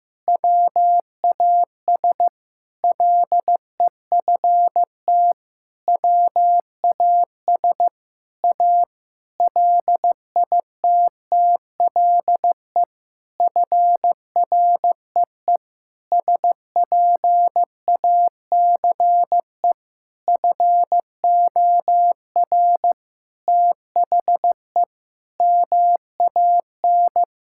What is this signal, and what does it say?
Morse code, heard as a keyed tone, about 15 wpm, 700 Hz. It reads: WAS LEFT WAS A LITTLE FREE SPACE FOR THE MAN
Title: Sentences from Top 1000 Words - Review Part 5 - 15wpm